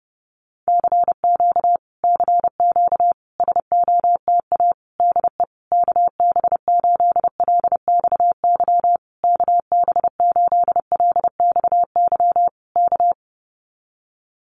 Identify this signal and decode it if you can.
Morse code, heard as a keyed tone, about 30 wpm, 700 Hz. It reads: CQ CQ SOTA DE KB8LXY KB8LXY K
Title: Common QSO Phrases Encore 30wpm